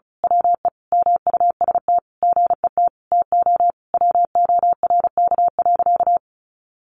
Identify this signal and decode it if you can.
Morse code, heard as a keyed tone, about 34 wpm, 700 Hz. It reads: WE MUST GET TO WORK.
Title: Sentences from Top 600 Words 35wpm